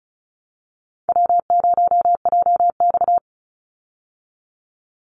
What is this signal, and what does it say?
Morse code, heard as a keyed tone, about 34 wpm, 700 Hz. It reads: W0JX
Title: Easy Call Signs Repeated 3x - 35wpm